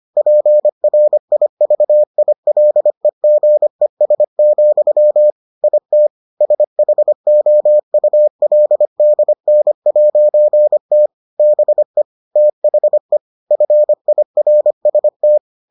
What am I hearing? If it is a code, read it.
Morse code, heard as a keyed tone, about 25 wpm, 600 Hz. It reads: PRIVILEGES, IT SHOULDN'T BE THE FIRST